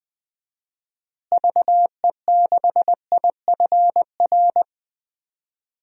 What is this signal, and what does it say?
Morse code, heard as a keyed tone, about 20 wpm, 700 Hz. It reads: VE6IFR
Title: DXCC-100 Call Signs - 6 Characters - Repeated 3x - 20wpm